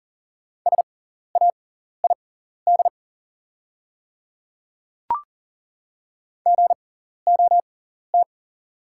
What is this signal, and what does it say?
Morse code, heard as a keyed tone, about 37 wpm, spaced as for 14 wpm, 690 Hz. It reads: SAID GOT
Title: 6x Character Spacing - Top 300 Words - 40wpm